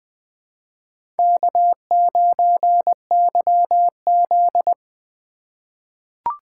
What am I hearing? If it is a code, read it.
Morse code, heard as a keyed tone, about 20 wpm, 700 Hz. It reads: K9YZ
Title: USA Call Signs - 3x Character Spacing - 20wpm